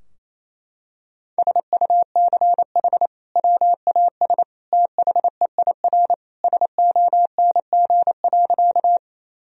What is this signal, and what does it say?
Morse code, heard as a keyed tone, about 28 wpm, 700 Hz. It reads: SUCH WAS THEIR SONG.
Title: Sentences from Top 1000 Words - Review Part 2 - 28wpm